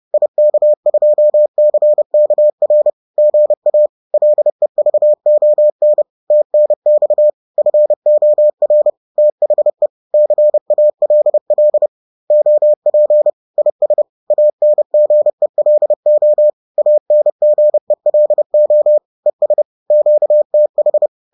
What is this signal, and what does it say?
Morse code, heard as a keyed tone, about 30 wpm, 600 Hz. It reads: IK2CKR GA LEVON TNX FOR THE CALL OP IS ANGELO ANGELO ES QTH